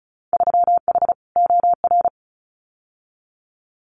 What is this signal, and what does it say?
Morse code, heard as a keyed tone, about 36 wpm, 700 Hz. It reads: 3H OR
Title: Winter Field Day Exchanges Encore 35wpm